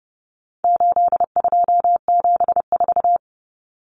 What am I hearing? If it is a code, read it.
Morse code, heard as a keyed tone, about 30 wpm, 700 Hz. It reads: 8274
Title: Numbers - 4-Digit 30wpm